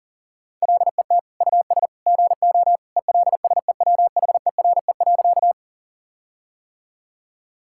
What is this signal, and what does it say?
Morse code, heard as a keyed tone, about 40 wpm, 700 Hz. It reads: LET US GO ELSEWHERE.
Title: Sentences from Top 1600 Words 40wpm